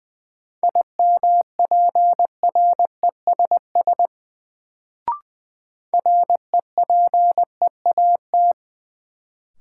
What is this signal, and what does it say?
Morse code, heard as a keyed tone, about 20 wpm, 700 Hz. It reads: IMPRESS REPEAT
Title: Sentences from Top 1600 Words - New Word Drill - 20wpm